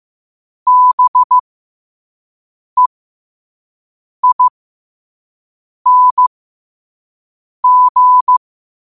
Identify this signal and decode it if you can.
Morse code, heard as a keyed tone, about 15 wpm, spaced as for 5 wpm, 1000 Hz. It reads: BEING
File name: qotdfiles / 5wpm